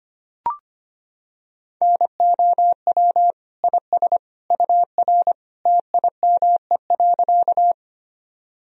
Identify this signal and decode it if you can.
Morse code, heard as a keyed tone, about 25 wpm, 700 Hz. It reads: NOW IS UR TIME.
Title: Sentences from Top 100 Words and Abbreviations 25wpm